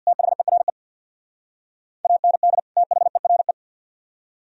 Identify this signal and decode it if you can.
Morse code, heard as a keyed tone, about 61 wpm, 700 Hz. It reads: THERE AND THERE
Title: Speed-Racing - Sets of 2 Words 50wpm